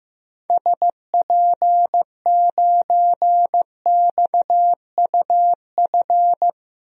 Morse code, heard as a keyed tone, about 15 wpm, 700 Hz. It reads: SP9XUF
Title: DXCC-100 Call Signs - 6 Characters - Repeated 2x - 15wpm